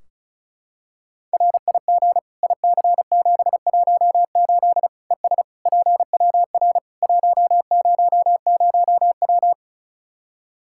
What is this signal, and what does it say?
Morse code, heard as a keyed tone, about 36 wpm, 700 Hz. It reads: RIG IC718 ES PWR 100W
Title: Practice QSO - Exchange 2 - Info - 35wpm